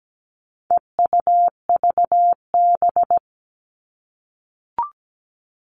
Morse code, heard as a keyed tone, about 17 wpm, 700 Hz. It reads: EUVB
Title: All Cut Numbers - 4-Digit 17wpm